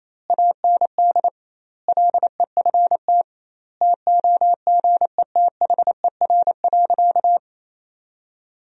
Morse code, heard as a keyed tone, about 28 wpm, 700 Hz. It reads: AND LEFT TOGETHER.
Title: 2x Word-Spacing - Sentences from Top 400 Words - 28wpm